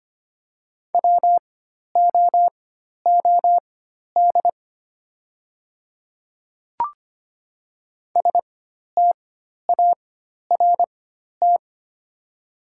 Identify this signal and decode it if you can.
Morse code, heard as a keyed tone, about 25 wpm, spaced as for 12 wpm, 700 Hz. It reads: WOOD START
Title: 4x Character Spacing - Top 500 Words - 25wpm